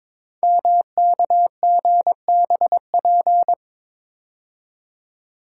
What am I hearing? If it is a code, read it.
Morse code, heard as a keyed tone, about 22 wpm, 700 Hz. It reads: MKGBP
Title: Code Groups - 5 Letters - 22wpm